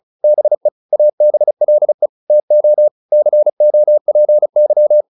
Morse code, heard as a keyed tone, about 35 wpm, 600 Hz. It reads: BE ABLE TO COPY